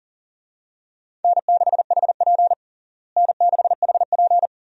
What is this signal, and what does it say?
Morse code, heard as a keyed tone, about 40 wpm, 700 Hz. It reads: N6HP N6HP
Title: USA Call Signs - 4 Characters - Repeated 2x - 40wpm